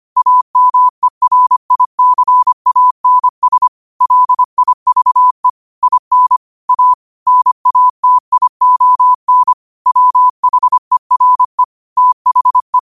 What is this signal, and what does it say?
Morse code, heard as a keyed tone, about 25 wpm, 1000 Hz. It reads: AMERICANS LIVE IN A NATION WHERE THE
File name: qotdfiles / 25wpm